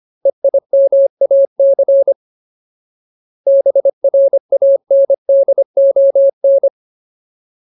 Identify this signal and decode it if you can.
Morse code, heard as a keyed tone, about 25 wpm, 550 Hz. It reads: EIMAC BRANDON
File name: Ultimate Word List / 25WPM Ultimate